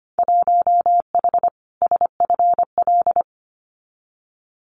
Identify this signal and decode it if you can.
Morse code, heard as a keyed tone, about 25 wpm, 700 Hz. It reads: 1H SFL
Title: Winter Field Day Exchanges 25wpm